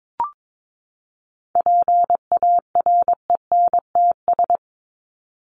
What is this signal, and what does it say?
Morse code, heard as a keyed tone, about 22 wpm, 700 Hz. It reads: PARENTS
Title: ES:S Suffix - Top 100 Words with Base Words 22wpm